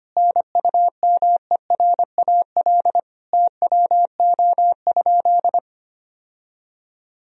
Morse code, heard as a keyed tone, about 25 wpm, 700 Hz. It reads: NUMERAL TWO?
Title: Sentences from Top 1000 Words - Review Part 2 - 25wpm